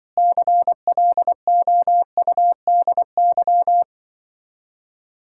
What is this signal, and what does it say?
Morse code, heard as a keyed tone, about 24 wpm, 700 Hz. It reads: CLOUDY
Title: Speed-Racing Common QSO Phrases - 20wpm